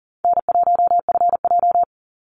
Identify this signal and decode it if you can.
Morse code, heard as a keyed tone, about 38 wpm, 700 Hz. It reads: N1FJ
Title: NA SOTA Activator Callsigns 40wpm